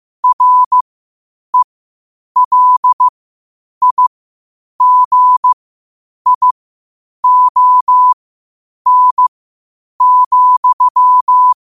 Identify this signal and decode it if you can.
Morse code, heard as a keyed tone, about 15 wpm, spaced as for 8 wpm, 1000 Hz. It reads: RELIGION,
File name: qotdfiles / 8wpm